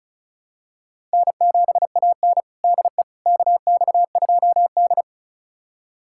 Morse code, heard as a keyed tone, about 36 wpm, 700 Hz. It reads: N7AN DE KX2D
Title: USA Callsign DE Callsign - 4 Characters - 35wpm